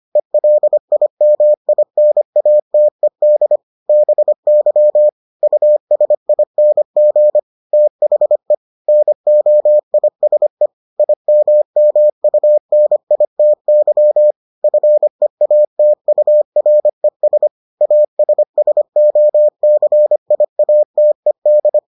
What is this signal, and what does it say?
Morse code, heard as a keyed tone, about 25 wpm, 600 Hz. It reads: ELIMINATED BY USING THE NOISE IMMUNITY FEATURES ASSOCIATED